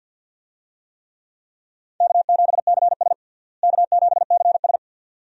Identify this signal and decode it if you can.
Morse code, heard as a keyed tone, about 51 wpm, 700 Hz. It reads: K7CS K7CS
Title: USA Call Signs - 4 Characters - Repeated 2x - 50wpm